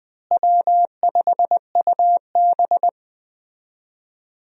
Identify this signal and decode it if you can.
Morse code, heard as a keyed tone, about 20 wpm, 700 Hz. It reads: W5UB
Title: USA Call Signs - 4 Characters - Repeated 2x - 20wpm